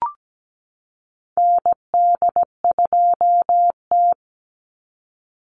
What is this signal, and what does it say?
Morse code, heard as a keyed tone, about 17 wpm, 700 Hz. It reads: ND2T
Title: Call Signs Repeated 2x - 17wpm